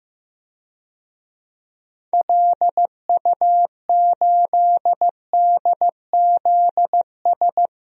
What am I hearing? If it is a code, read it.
Morse code, heard as a keyed tone, about 15 wpm, 700 Hz. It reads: LU8DZS
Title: DXCC-100 Call Signs - 6 Characters - 15wpm